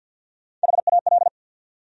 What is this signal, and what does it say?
Morse code, heard as a keyed tone, about 53 wpm, 700 Hz. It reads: HAL
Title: USA Common Names - Max 5 Letters - 50wpm